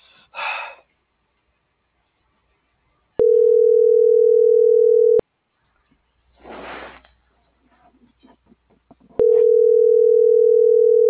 Ringback tone (North America), 440 + 480 Hz.